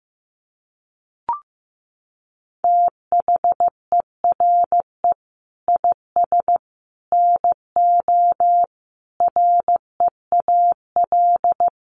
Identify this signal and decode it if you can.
Morse code, heard as a keyed tone, about 15 wpm, 700 Hz. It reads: THERE IS NO REAL
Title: Sentences from Top 1000 Words 15wpm